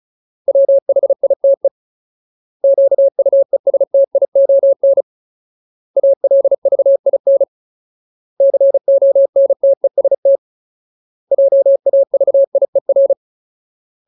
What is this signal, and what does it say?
Morse code, heard as a keyed tone, about 34 wpm, 550 Hz. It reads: WHITE QUESTION ALVIN CONTEST JAVIER
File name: Ultimate Word List / 35WPM Ultimate